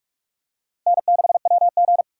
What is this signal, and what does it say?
Morse code, heard as a keyed tone, about 45 wpm, 700 Hz. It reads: N6WG
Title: K1USN SST QSO Practice - 45wpm